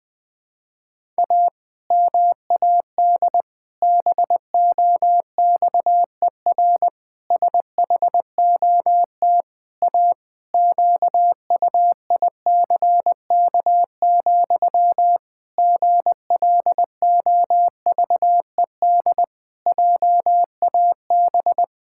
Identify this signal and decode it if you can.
Morse code, heard as a keyed tone, about 20 wpm, 700 Hz. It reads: A MAD BOXER SHOT A QUICK, GLOVED JAB
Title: Pangram Challenge v2 - 20wpm